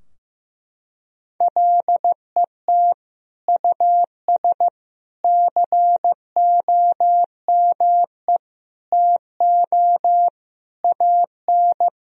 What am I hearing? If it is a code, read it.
Morse code, heard as a keyed tone, about 15 wpm, 700 Hz. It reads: LET US COME TO AN
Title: Sentences from Top 1000 Words - Review Part 3 - 15wpm